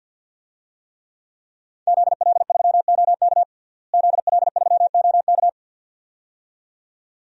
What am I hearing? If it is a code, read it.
Morse code, heard as a keyed tone, about 49 wpm, 700 Hz. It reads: ZL3OK ZL3OK
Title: DXCC-100 Call Signs - 5 Characters - Repeated 2x - 50wpm